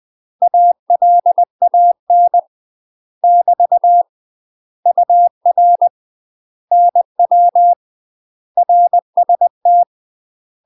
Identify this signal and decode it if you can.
Morse code, heard as a keyed tone, about 20 wpm, 700 Hz. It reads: ALAN <BT> UR NW RST
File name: UK MASTER QSOs TRAINING / 20WPM / UK MASTER QSOs 0047